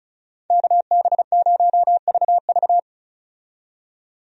Morse code, heard as a keyed tone, about 36 wpm, 700 Hz. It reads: KB0VV
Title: USA Call Signs - 5 Characters - 35wpm